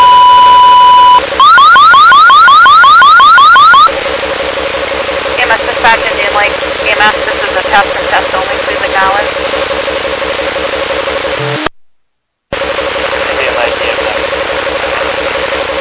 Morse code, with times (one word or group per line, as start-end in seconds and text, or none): none